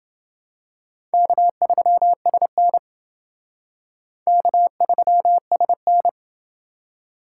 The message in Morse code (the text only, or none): K3SN K3SN